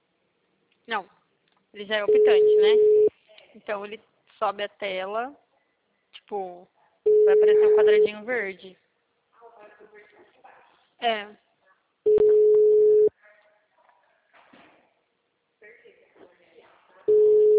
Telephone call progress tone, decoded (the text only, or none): ringback tone